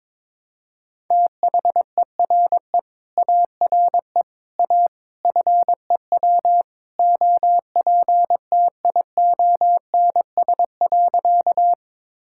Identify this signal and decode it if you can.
Morse code, heard as a keyed tone, about 22 wpm, 700 Hz. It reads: THERE ARE A FEW OPTIONS.
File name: practice / 22wpm / Sentences with 3-Word Phrases - 22wpm